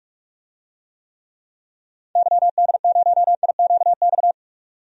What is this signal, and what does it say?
Morse code, heard as a keyed tone, about 42 wpm, 700 Hz. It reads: YD0IQX
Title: DXCC-100 Call Signs - 6 Characters - 45wpm